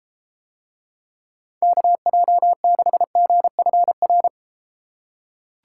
Morse code, heard as a keyed tone, about 32 wpm, 700 Hz. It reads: KJ6GFR